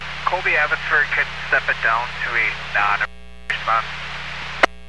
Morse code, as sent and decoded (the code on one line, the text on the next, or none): none